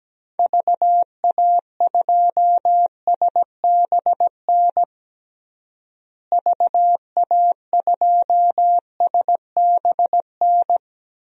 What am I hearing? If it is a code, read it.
Morse code, heard as a keyed tone, about 17 wpm, 700 Hz. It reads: VA2SBN VA2SBN
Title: DXCC-100 Call Signs - 6 Characters - Repeated 2x - 17wpm